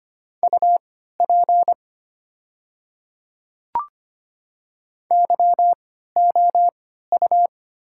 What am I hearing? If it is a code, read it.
Morse code, heard as a keyed tone, about 25 wpm, spaced as for 14 wpm, 700 Hz. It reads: UP YOU
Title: 3x Character Spacing - Top 100 Words - 25wpm